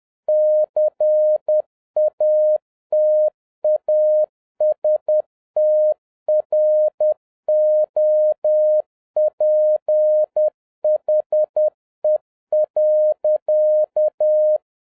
Morse code, heard as a keyed tone, about 10 wpm, 610 Hz. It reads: CATASTROPHE.
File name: 2011-07-mp3 / n31 THE 3.11 2 10wpm